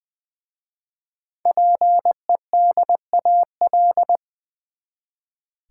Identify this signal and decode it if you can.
Morse code, heard as a keyed tone, about 20 wpm, 700 Hz. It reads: PEDAL
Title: Easy Words with Hints - 20wpm